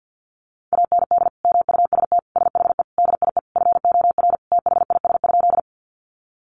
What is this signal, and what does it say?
Morse code, heard as a keyed tone, about 50 wpm, 700 Hz. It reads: AND MUST SHE DIE FOR THIS?
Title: Sentences from Top 1300 Words 50wpm